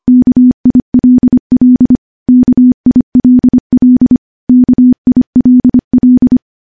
Morse code, heard as a keyed tone, about 25 wpm, 250 Hz. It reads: KILL KILL KILL
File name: morse / morse-training-25WPM